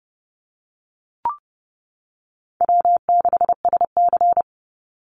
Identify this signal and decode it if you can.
Morse code, heard as a keyed tone, about 30 wpm, 700 Hz. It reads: W6SC
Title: Easy Call Signs - 30wpm